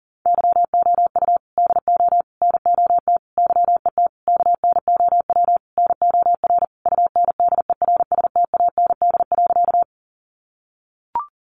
Morse code, heard as a keyed tone, about 37 wpm, 700 Hz. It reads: YOU DO NOT YET KNOW NOR UNDERSTAND.
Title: Sentences from Top 1500 Words 40wpm